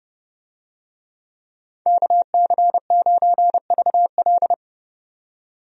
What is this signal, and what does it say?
Morse code, heard as a keyed tone, about 30 wpm, 700 Hz. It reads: KC9VL
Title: USA Call Signs - 5 Characters - 30wpm